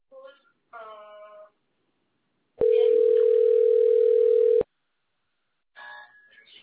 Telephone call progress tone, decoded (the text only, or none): ringback tone